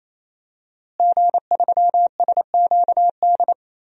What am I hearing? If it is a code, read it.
Morse code, heard as a keyed tone, about 28 wpm, 700 Hz. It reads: G3SQD
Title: DXCC-100 Call Signs - 5 Characters - Repeated 3x - 28wpm